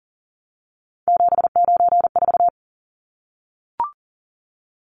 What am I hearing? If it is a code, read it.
Morse code, heard as a keyed tone, about 40 wpm, 700 Hz. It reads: 794
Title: Numbers - 3-Digit 40wpm